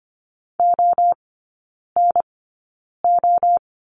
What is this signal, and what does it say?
Morse code, heard as a keyed tone, about 25 wpm, spaced as for 9 wpm, 700 Hz. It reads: ONO
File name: Lesson 17e - Letter u - Three Characters 25wpm